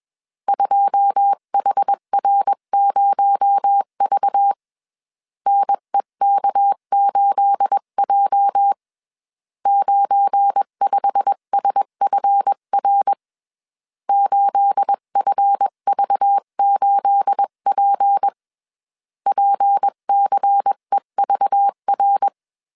Morse code, heard as a keyed tone, about 21 wpm, 800 Hz. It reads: 2HR0V NEK8J 95SFR 8FV8P PCEVR